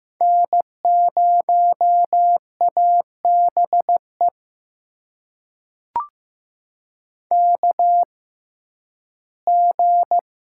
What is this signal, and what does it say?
Morse code, heard as a keyed tone, about 15 wpm, spaced as for 11 wpm, 700 Hz. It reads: N0ABE KG
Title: USA Call Signs - 5 Characters - 6x Character Spacing - 15wpm